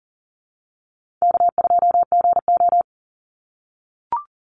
Morse code, heard as a keyed tone, about 40 wpm, 700 Hz. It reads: K2GO
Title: Easy Call Signs - 40wpm